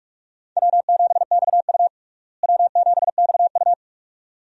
Morse code, heard as a keyed tone, about 44 wpm, 700 Hz. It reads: W7XU W7XU
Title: USA Call Signs - 4 Characters - Repeated 2x - 45wpm